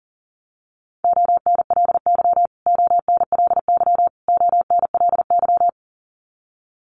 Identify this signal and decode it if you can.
Morse code, heard as a keyed tone, about 40 wpm, 700 Hz. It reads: ONLY ONLY ONLY
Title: Top 300 Words Repeated 40wpm